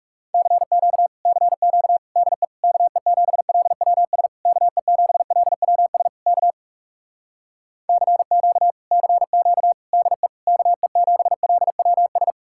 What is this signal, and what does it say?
Morse code, heard as a keyed tone, about 37 wpm, 700 Hz. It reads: CQ CQ DE KE7LWS KE7LWS K CQ CQ DE KE7LWS
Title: Speed-Racing Common QSO Phrases - 30wpm